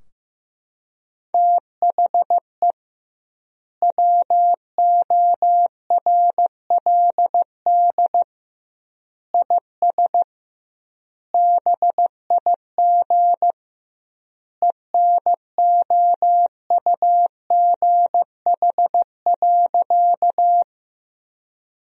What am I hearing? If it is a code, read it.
Morse code, heard as a keyed tone, about 15 wpm, 700 Hz. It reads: THE WORLD IS BIG ENOUGH.
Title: 2x Word-Spacing - Sentences from Top 400 Words - 15wpm